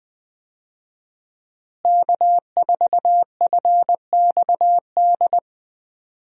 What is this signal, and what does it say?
Morse code, heard as a keyed tone, about 21 wpm, 700 Hz. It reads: K4FXD
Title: USA Call Signs - 5 Characters - Repeated 3x - 20wpm